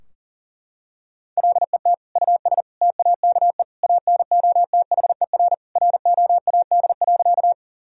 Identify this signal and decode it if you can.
Morse code, heard as a keyed tone, about 37 wpm, 700 Hz. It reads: LET US TAKE ANOTHER ROAD.
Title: Sentences from Top 1000 Words - Review Part 2 - 40wpm